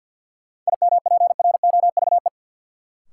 Morse code, heard as a keyed tone, about 51 wpm, 700 Hz. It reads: IMPROVE
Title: Sentences from Top 800 Words - New Word Drill - 50wpm